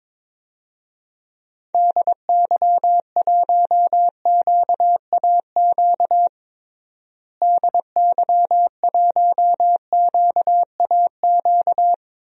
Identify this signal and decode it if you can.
Morse code, heard as a keyed tone, about 22 wpm, 700 Hz. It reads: DY1QAQ DY1QAQ
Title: DXCC-100 Call Signs - 6 Characters - Repeated 2x - 22wpm